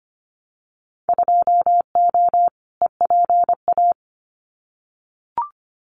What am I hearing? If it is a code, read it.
Morse code, heard as a keyed tone, about 25 wpm, 700 Hz. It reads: 2O EPA
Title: Winter Field Day Exchanges Encore 25wpm